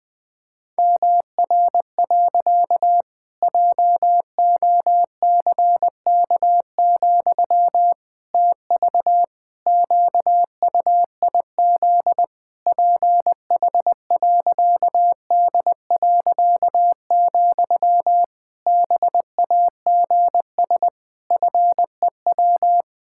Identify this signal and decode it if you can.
Morse code, heard as a keyed tone, about 21 wpm, 700 Hz. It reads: MR. JOCK, TV QUIZ PH.D., BAGS FEW
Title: Pangram Challenge v2 - 20wpm